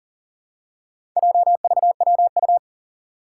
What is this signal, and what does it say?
Morse code, heard as a keyed tone, about 40 wpm, 700 Hz. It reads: JVWU
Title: Four Challenging Letter-Number Combinations 40wpm